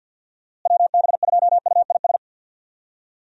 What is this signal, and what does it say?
Morse code, heard as a keyed tone, about 52 wpm, 700 Hz. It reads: WB2UIS
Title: USA Call Signs - 6 Characters - Repeated 2x - 50wpm